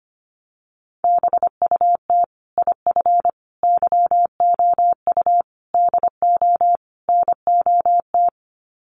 Morse code, heard as a keyed tone, about 25 wpm, 700 Hz. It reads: BUT IF YOU DO NOT
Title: Sets of 5 Words Encore 25wpm